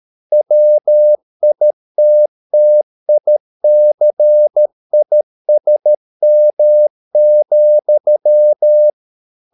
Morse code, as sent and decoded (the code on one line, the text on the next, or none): .-- .. - - .. -.-. .. ... -- --..--
WITTICISM,